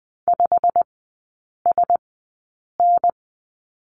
Morse code, heard as a keyed tone, about 20 wpm, spaced as for 8 wpm, 700 Hz. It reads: 5SN
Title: 087 Lesson 15e - Number 5 - Three Characters 20wpm